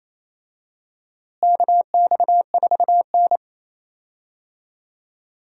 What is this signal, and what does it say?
Morse code, heard as a keyed tone, about 28 wpm, 700 Hz. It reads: KX4N
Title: USA Call Signs - 4 Characters - Repeated 2x - 28wpm